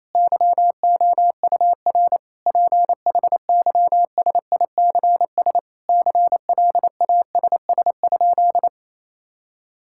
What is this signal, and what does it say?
Morse code, heard as a keyed tone, about 28 wpm, 700 Hz. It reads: YOUR PHYSICS CLASS?
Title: Sentences from Top 1000 Words - Review Part 5 - 28wpm